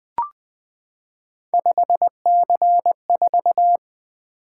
Morse code, heard as a keyed tone, about 20 wpm, 700 Hz. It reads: 5C4
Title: Three Letter-Number Combinations - 20wpm